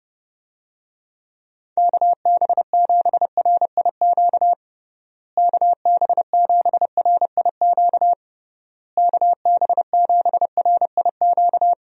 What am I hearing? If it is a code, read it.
Morse code, heard as a keyed tone, about 32 wpm, 700 Hz. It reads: KB7RIQ KB7RIQ KB7RIQ
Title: USA Call Signs - 6 Characters - Repeated 3x - 30wpm